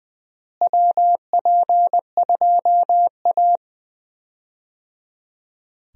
Morse code, heard as a keyed tone, about 20 wpm, 700 Hz. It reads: WP2A
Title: USA Call Signs - 4 Characters - Repeated 3x - 20wpm